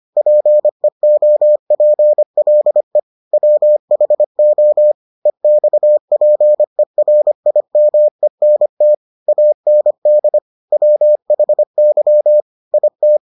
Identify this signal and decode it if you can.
Morse code, heard as a keyed tone, about 25 wpm, 600 Hz. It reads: PEOPLE WHO EXPERIMENT AND WHY IT